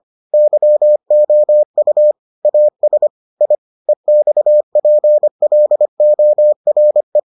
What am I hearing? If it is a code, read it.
Morse code, heard as a keyed tone, about 25 wpm, 600 Hz. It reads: YOU AS I EXPLORE